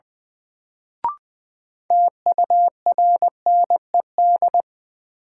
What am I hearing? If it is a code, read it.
Morse code, heard as a keyed tone, about 20 wpm, 700 Hz. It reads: TURNED
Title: Sets of 4 Words - New Word Drill - 20wpm